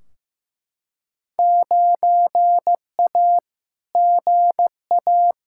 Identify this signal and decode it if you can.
Morse code, heard as a keyed tone, about 15 wpm, 700 Hz. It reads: 9A GA
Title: Field Day Exchanges Encore - 15wpm